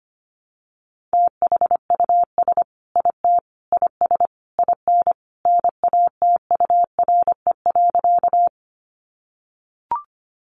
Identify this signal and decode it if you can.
Morse code, heard as a keyed tone, about 25 wpm, 700 Hz. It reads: THUS IT IS IN NATURE.
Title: Sentences from Top 800 Words 25wpm